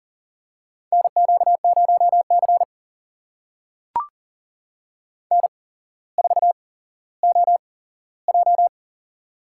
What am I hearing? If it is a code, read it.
Morse code, heard as a keyed tone, about 38 wpm, spaced as for 17 wpm, 700 Hz. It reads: NQ0C N4OJ
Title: USA Call Signs - 4 Characters - 8x Character Spacing - 40wpm